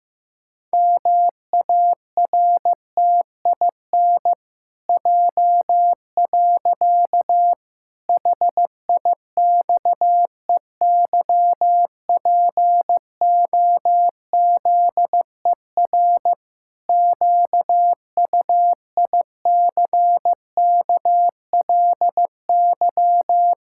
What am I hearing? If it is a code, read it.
Morse code, heard as a keyed tone, about 15 wpm, 700 Hz. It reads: MARTIN J. HIXEYPOZER QUICKLY
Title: Pangram Challenge v2 - 15wpm